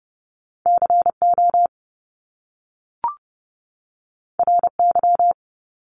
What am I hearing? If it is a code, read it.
Morse code, heard as a keyed tone, about 30 wpm, 700 Hz. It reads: CO RY
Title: Top Two Letter Combinations in Words 30wpm